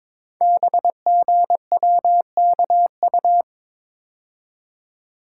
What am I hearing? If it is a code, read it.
Morse code, heard as a keyed tone, about 22 wpm, 690 Hz. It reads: BGWKU
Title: Code Groups - 5 Letters - 22wpm